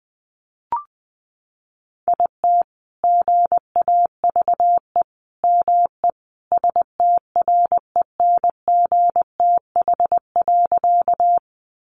Morse code, heard as a keyed tone, about 20 wpm, 700 Hz. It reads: IT GAVE ME STRENGTH.
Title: Sentences from Top 1300 Words 20wpm